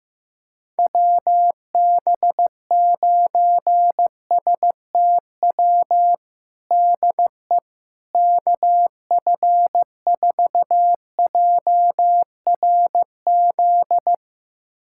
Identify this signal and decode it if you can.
Morse code, heard as a keyed tone, about 15 wpm, 700 Hz. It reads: WB9STW DE KF4JRZ
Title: USA Callsign DE Callsign 15wpm